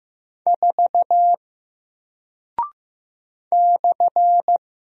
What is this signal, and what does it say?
Morse code, heard as a keyed tone, about 15 wpm, 700 Hz. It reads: 4 /
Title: Single Character (Letter:Number:Punctuation:Prosign) Alt 15wpm